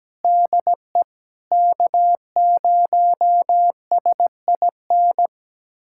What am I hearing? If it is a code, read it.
Morse code, heard as a keyed tone, about 17 wpm, 700 Hz. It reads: DE K0SIN
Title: USA Callsign DE Callsign - 5 Characters - 17wpm